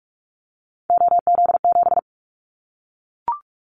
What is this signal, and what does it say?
Morse code, heard as a keyed tone, about 44 wpm, 700 Hz. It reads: OZ7
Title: Callsign Prefixes for Top 100 DXCC - 45wpm